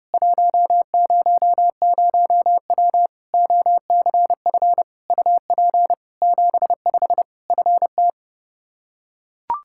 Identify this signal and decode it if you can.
Morse code, heard as a keyed tone, about 30 wpm, 700 Hz. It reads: 100W OCF UP 75 FT
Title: K1USN SST QSO Practice 30wpm